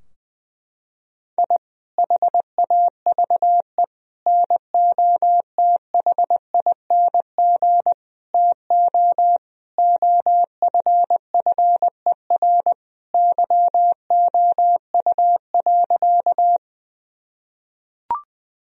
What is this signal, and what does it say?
Morse code, heard as a keyed tone, about 20 wpm, 700 Hz. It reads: I HAVE NOTHING TO OFFER YOU.
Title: Sentences from Top 1000 Words - Review Part 4 - 20wpm